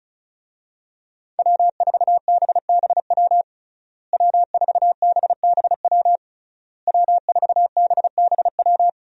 Morse code, heard as a keyed tone, about 36 wpm, 700 Hz. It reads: W4BBW W4BBW W4BBW
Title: USA Call Signs - 5 Characters - Repeated 3x - 35wpm